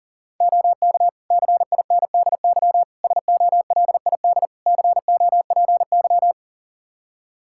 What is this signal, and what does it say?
Morse code, heard as a keyed tone, about 37 wpm, 700 Hz. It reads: OK CINDY SOLID COPY
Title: K1USN SST QSO Practice 40wpm